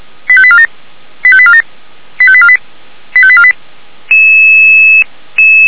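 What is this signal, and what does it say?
Five-tone selective call (ZVEI), heard as quick runs of five tones, tones of about 70 ms. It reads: 85847 85847 85848 85848
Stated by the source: ZVEI five-tone selective call on 464, received by SDR.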